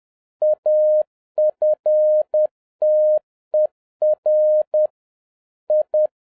Morse code, heard as a keyed tone, about 10 wpm, 610 Hz. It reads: AFTER I